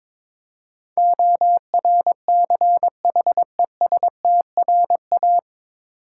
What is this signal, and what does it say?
Morse code, heard as a keyed tone, about 22 wpm, 700 Hz. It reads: ORCHESTRA
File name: 9 Letter Words - First 3 Letters Given - 22wpm